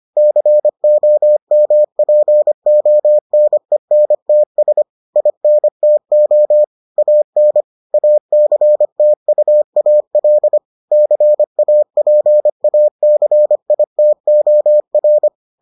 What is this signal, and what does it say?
Morse code, heard as a keyed tone, about 25 wpm, 600 Hz. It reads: COMPONENTS INTO AN ACTUAL CAPACITOR